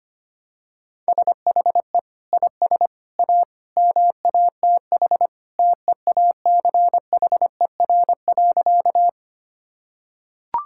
Morse code, heard as a keyed tone, about 25 wpm, 700 Hz. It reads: SHE IS A MATH TEACHER.
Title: Sentences from Top 1000 Words - Review Part 6 - 25wpm